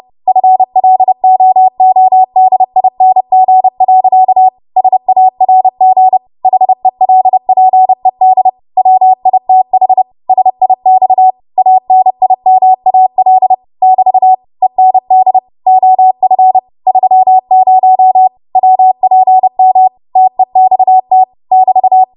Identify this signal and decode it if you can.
Morse code, heard as a keyed tone, about 30 wpm, 750 Hz. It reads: FLOODING. SARG HELPED WITH SIX ANIMAL <BT> END OF 30 WPM TEXT <BT>